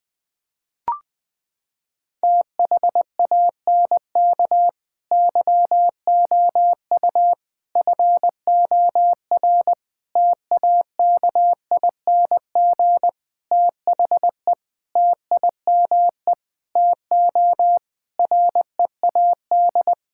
Morse code, heard as a keyed tone, about 21 wpm, 700 Hz. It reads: THANK YOU FOR TAKING THE TIME TO READ
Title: Sets of 9 Words - 20wpm